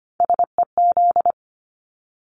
Morse code, heard as a keyed tone, about 25 wpm, 700 Hz. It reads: SEZ
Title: Abbreviations Repeated 3x - 25wpm